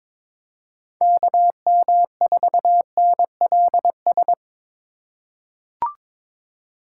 Morse code, heard as a keyed tone, about 22 wpm, 700 Hz. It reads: KM4NLS
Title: USA Call Signs - 6 Characters - 6x Character Spacing - 22wpm